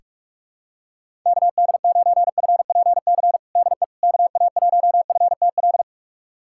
Morse code, heard as a keyed tone, about 46 wpm, 700 Hz. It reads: KD9FPC DE KA1FTL
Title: USA Callsign DE Callsign - 6 Characters - 45wpm